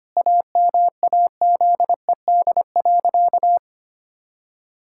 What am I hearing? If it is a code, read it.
Morse code, heard as a keyed tone, about 25 wpm, 700 Hz. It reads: AMAZED.
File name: practice / 25wpm / Sentences with 4-Word Phrases - 25wpm